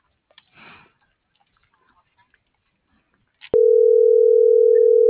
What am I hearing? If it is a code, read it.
Ringback tone (North America), 440 + 480 Hz.
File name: CENTRAL MULTIMODAL / out alfredovelazquez - .71983